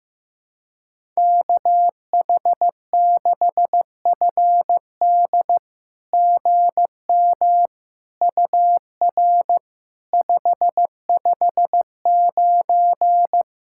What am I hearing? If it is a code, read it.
Morse code, heard as a keyed tone, about 15 wpm, 700 Hz. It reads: KH6FD GM UR 559
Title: SOTA QSO Practice 15wpm